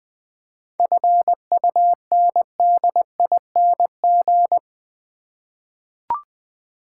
Morse code, heard as a keyed tone, about 20 wpm, 690 Hz. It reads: FUNDING
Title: 7 Letter Words - First 3 Letters Given - 20wpm